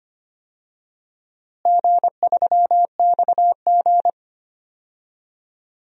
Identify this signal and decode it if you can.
Morse code, heard as a keyed tone, about 25 wpm, 700 Hz. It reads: G3XG
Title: DXCC-100 Call Signs - 4 Characters - Repeated 3x - 25wpm